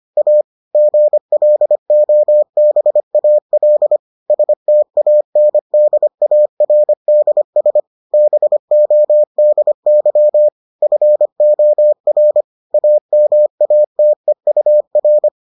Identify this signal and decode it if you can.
Morse code, heard as a keyed tone, about 25 wpm, 600 Hz. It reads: A GLOBAL STANDARDS BODY FOR AMATEUR